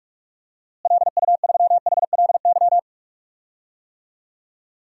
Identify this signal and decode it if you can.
Morse code, heard as a keyed tone, about 46 wpm, 700 Hz. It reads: LU3HLY
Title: DXCC-100 Call Signs - 6 Characters - Repeated 3x - 45wpm